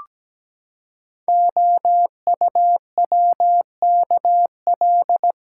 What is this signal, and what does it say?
Morse code, heard as a keyed tone, about 17 wpm, 700 Hz. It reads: OUWKL
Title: Code Groups - 5 Letters - 17wpm